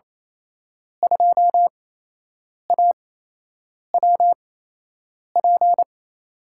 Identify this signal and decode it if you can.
Morse code, heard as a keyed tone, about 28 wpm, spaced as for 8 wpm, 700 Hz. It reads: 2AWP
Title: USA Call Signs - 5 Characters - 8x Character Spacing - 28wpm